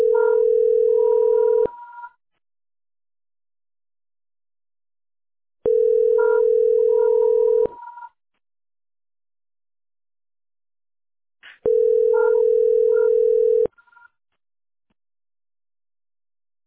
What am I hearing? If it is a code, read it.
Ringback tone (North America), 440 + 480 Hz.
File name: e78029de-100e-4318-9a57